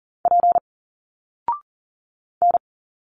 Morse code, heard as a keyed tone, about 40 wpm, 700 Hz. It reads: P N
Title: Single Letters 40wpm